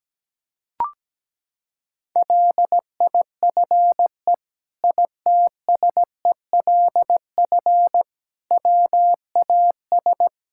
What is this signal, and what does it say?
Morse code, heard as a keyed tone, about 17 wpm, 700 Hz. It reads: LIFE ITSELF WAS
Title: Sentences from Top 1000 Words - Review Part 5 - 17wpm